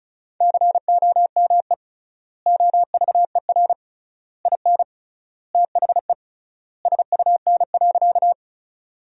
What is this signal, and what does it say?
Morse code, heard as a keyed tone, about 36 wpm, 700 Hz. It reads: COME OVER IN THE SUN.
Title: 3x Word-Spacing - Sentences from Top 300 Words - 35wpm